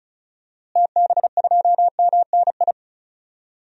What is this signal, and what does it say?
Morse code, heard as a keyed tone, about 34 wpm, 700 Hz. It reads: TB2MNI
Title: DXCC-100 Call Signs - 6 Characters - Repeated 2x - 35wpm